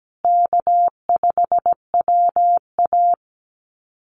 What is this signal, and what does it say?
Morse code, heard as a keyed tone, about 17 wpm, 700 Hz. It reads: K5WA
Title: Easy Call Signs Repeated 2x - 17wpm